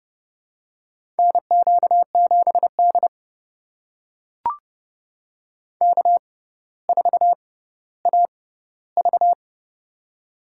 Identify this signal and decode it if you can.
Morse code, heard as a keyed tone, about 29 wpm, spaced as for 15 wpm, 700 Hz. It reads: NQ7D K4AV
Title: USA Call Signs - 4 Characters - 6x Character Spacing - 30wpm